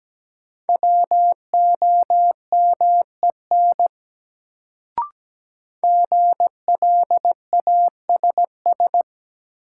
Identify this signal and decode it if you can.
Morse code, heard as a keyed tone, about 17 wpm, 700 Hz. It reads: WOMEN GLASS
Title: Sentences from Top 1300 Words - New Word Drill Alt - 17wpm